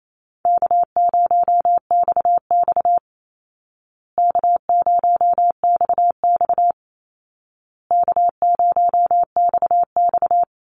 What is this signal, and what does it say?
Morse code, heard as a keyed tone, about 28 wpm, 700 Hz. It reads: K0XX K0XX K0XX
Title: Easy Call Signs Repeated 3x - 28wpm